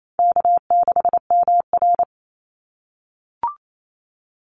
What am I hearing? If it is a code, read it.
Morse code, heard as a keyed tone, about 28 wpm, 700 Hz. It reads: K6MR
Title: Call Signs - 28wpm